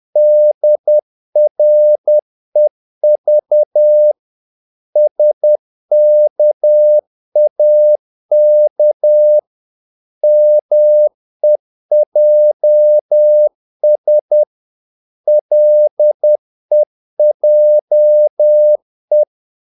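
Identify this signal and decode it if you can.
Morse code, heard as a keyed tone, about 10 wpm, 600 Hz. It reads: DREV SKAK MEJS LEJE
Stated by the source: Morse practice of short Danish words at 10 wpm, from CW med Gnister.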